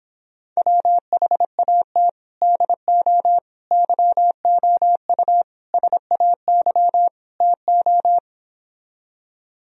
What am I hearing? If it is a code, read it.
Morse code, heard as a keyed tone, about 26 wpm, 700 Hz. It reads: WHAT DO YOU SAY TO